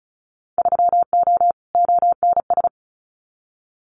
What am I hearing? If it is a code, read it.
Morse code, heard as a keyed tone, about 34 wpm, 700 Hz. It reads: 3O ONS
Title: Winter Field Day Exchanges Encore 35wpm